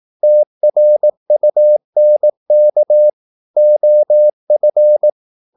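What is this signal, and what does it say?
Morse code, heard as a keyed tone, about 18 wpm, 600 Hz. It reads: TRUNK OF
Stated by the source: Morse practice at 18 wpm, English text, from CW med Gnister.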